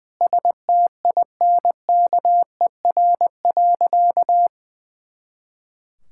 Morse code, heard as a keyed tone, about 19 wpm, 700 Hz. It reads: STINKER.
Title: Pangram Challenge v2 - 20wpm